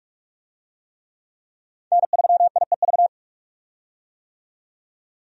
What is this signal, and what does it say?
Morse code, heard as a keyed tone, about 46 wpm, 700 Hz. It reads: N3IEV
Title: USA Call Signs - 5 Characters - 45wpm